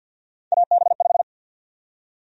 Morse code, heard as a keyed tone, about 51 wpm, 700 Hz. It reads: AB5
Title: USA Call Sign Prefixes - 50wpm